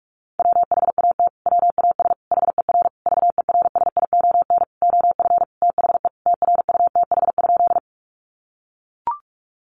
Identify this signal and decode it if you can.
Morse code, heard as a keyed tone, about 44 wpm, 700 Hz. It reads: WHAT WAS HER VERSION OF THE TRUTH?
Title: Sentences from Top 1400 Words 45wpm